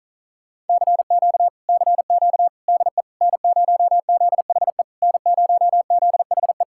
Morse code, heard as a keyed tone, about 47 wpm, 700 Hz. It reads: CQ CQ DE N0ZHE N0ZHE